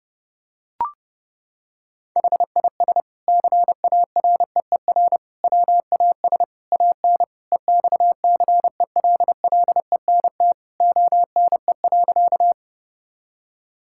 Morse code, heard as a keyed tone, about 30 wpm, 700 Hz. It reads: HIS CAREER WAS AN EXCELLENT ONE.
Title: Sentences from Top 1000 Words - Review Part 6 - 30wpm